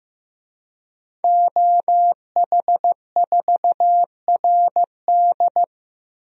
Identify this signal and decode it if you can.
Morse code, heard as a keyed tone, about 15 wpm, 700 Hz. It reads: OH4RD
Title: DXCC-100 Call Signs - 5 Characters - Repeated 3x - 15wpm